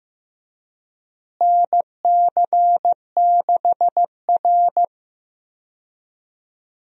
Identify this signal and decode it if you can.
Morse code, heard as a keyed tone, about 15 wpm, 700 Hz. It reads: NC6R
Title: USA Call Signs - 4 Characters - Repeated 3x - 15wpm